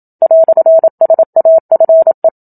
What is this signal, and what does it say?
Morse code, heard as a keyed tone, about 27 wpm, 650 Hz. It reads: "SAFE
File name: Book Two / 01-Chapter 1 - 27 WPM